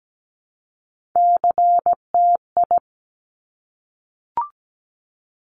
Morse code, heard as a keyed tone, about 17 wpm, 700 Hz. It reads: CTI
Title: Top Three Letter Combinations in Words 17wpm